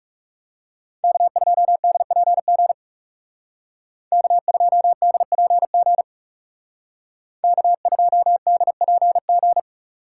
Morse code, heard as a keyed tone, about 34 wpm, 700 Hz. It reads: K2DPG K2DPG K2DPG
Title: Speed Racing - USA Call Signs 30wpm